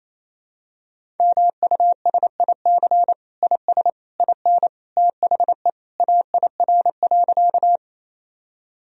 Morse code, heard as a keyed tone, about 28 wpm, 700 Hz. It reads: MUSIC IS IN THE AIR.
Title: Sentences from Top 1000 Words - Review Part 1 - 28wpm